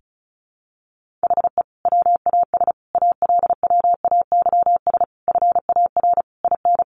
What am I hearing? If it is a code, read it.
Morse code, heard as a keyed tone, about 34 wpm, 700 Hz. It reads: HE WAS ALWAYS FAR IN
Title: Sentences from Top 1400 Words 35wpm